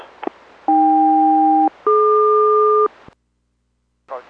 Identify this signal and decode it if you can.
Aircraft SELCAL, heard as two pairs of tones. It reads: AK-DQ